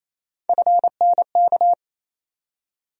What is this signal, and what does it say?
Morse code, heard as a keyed tone, about 28 wpm, 700 Hz. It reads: FNK
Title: USA Call Sign Postfixes - 3 Letters 28wpm